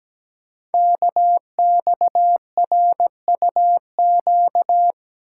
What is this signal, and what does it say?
Morse code, heard as a keyed tone, about 17 wpm, 700 Hz. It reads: KXRUQ
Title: Code Groups - 5 Letters - 17wpm